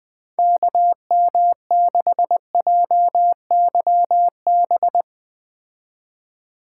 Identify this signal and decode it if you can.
Morse code, heard as a keyed tone, about 20 wpm, 700 Hz. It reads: KM6JYB
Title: Speed Racing - USA 2x3 Call Signs 17wpm